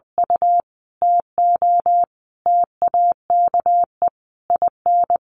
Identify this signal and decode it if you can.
Morse code, heard as a keyed tone, about 20 wpm, 700 Hz. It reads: U TO TAKE IN
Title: Sentences from Top 1500 Words 20wpm